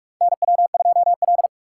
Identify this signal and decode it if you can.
Morse code, heard as a keyed tone, about 46 wpm, 700 Hz. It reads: NW2L
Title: USA Call Signs - 4 Characters - 3x Character Spacing - 45wpm